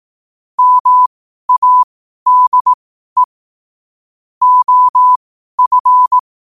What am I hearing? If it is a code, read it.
Morse code, heard as a keyed tone, about 18 wpm, spaced as for 12 wpm, 1000 Hz. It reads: MADE OF
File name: qotdfiles / 12wpm